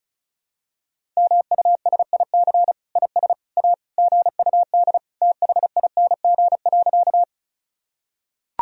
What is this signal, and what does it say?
Morse code, heard as a keyed tone, about 35 wpm, 700 Hz. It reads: MUSIC IS A GUD THING.
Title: Sentences from Top 300 Words and Abbreviations 35wpm